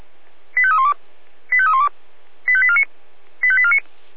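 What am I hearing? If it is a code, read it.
Five-tone selective call (ZVEI), heard as quick runs of five tones, tones of about 80 ms. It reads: 85312 85312 85849 85849
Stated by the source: ZVEI five-tone selective call received on an SDR.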